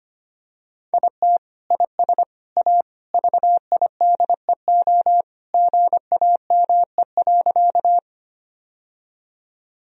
Morse code, heard as a keyed tone, about 25 wpm, 700 Hz. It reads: IT IS A VIDEO GAME.
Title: Sentences from Top 1000 Words - Review Part 4 - 25wpm